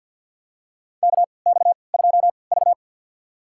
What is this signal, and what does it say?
Morse code, heard as a keyed tone, about 47 wpm, spaced as for 28 wpm, 700 Hz. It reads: KX2V
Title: USA Call Signs - 3x Character Spacing - 50wpm